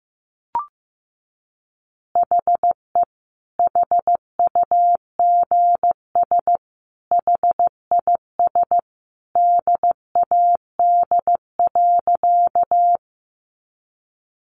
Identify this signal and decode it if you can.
Morse code, heard as a keyed tone, about 15 wpm, 700 Hz. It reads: HE HUGS HIS DAD.
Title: Easy Phrases 15wpm